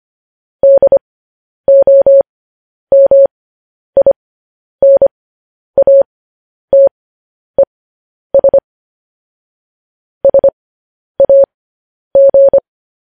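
Morse code, heard as a keyed tone, about 25 wpm, spaced as for 10 wpm, 550 Hz. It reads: DOMINATES SAG
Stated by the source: Morse code news headlines transmission.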